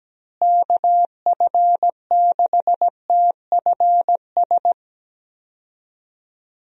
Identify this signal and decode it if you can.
Morse code, heard as a keyed tone, about 17 wpm, 700 Hz. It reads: KF6TFS
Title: USA Call Signs - 6 Characters - Repeated 2x - 17wpm